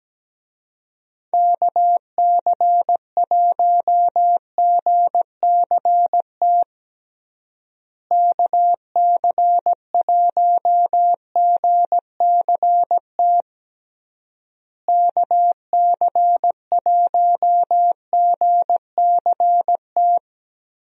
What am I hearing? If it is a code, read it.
Morse code, heard as a keyed tone, about 17 wpm, 700 Hz. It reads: KC1GCT KC1GCT KC1GCT
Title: USA Call Signs - 6 Characters - Repeated 3x - 17wpm